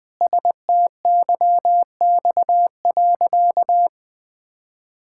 Morse code, heard as a keyed tone, about 20 wpm, 700 Hz. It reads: STYX.
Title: Pangram Challenge v2 - 20wpm